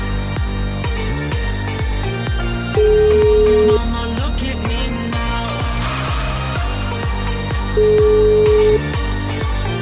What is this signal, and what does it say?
Ringback tone (Europe, 425 Hz), with noise.